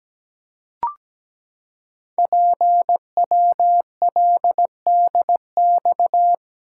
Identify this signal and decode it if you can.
Morse code, heard as a keyed tone, about 17 wpm, 700 Hz. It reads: PWLDX
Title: Code Groups - 5 Letter:Numbers - 17wpm